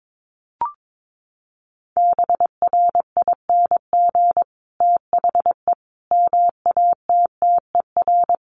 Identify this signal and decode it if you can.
Morse code, heard as a keyed tone, about 22 wpm, 700 Hz. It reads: BRING THE MATTER
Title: Sentences from Top 1300 Words 22wpm